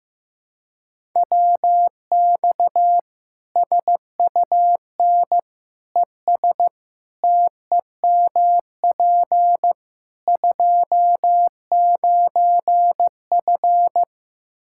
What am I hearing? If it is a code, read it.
Morse code, heard as a keyed tone, about 16 wpm, 700 Hz. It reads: WX SUN ES TEMP 29F
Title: K1USN SST QSO Practice 15wpm